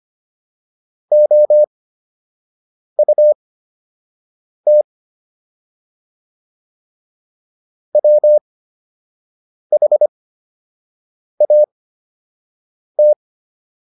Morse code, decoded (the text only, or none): OUT WHAT